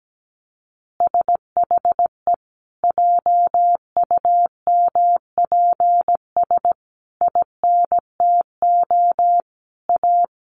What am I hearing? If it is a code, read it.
Morse code, heard as a keyed tone, about 17 wpm, 700 Hz. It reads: SHE JUMPS INTO A
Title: Additional Easy Phrases 17wpm